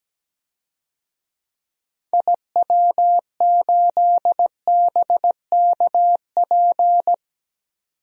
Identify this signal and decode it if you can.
Morse code, heard as a keyed tone, about 17 wpm, 700 Hz. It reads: IW8BKP
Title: DXCC-100 Call Signs - 6 Characters - 17wpm